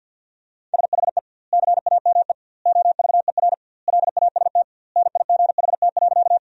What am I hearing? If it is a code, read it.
Morse code, heard as a keyed tone, about 50 wpm, 700 Hz. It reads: SHE CAME OVER LAST NIGHT.
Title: 2x Word-Spacing - Sentences from Top 300 Words - 50wpm